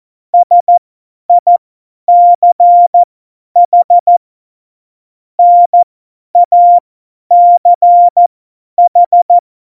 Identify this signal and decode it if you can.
Morse code, heard as a keyed tone, about 14 wpm, spaced as for 10 wpm, 700 Hz. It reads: SICH NACH
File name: Morsekurs / Witze / Witz 18. September 2018, 50 F70